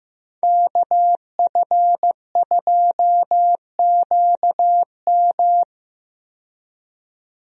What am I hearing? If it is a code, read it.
Morse code, heard as a keyed tone, about 15 wpm, 700 Hz. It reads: KF2QM
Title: USA Call Signs - 5 Characters - 15wpm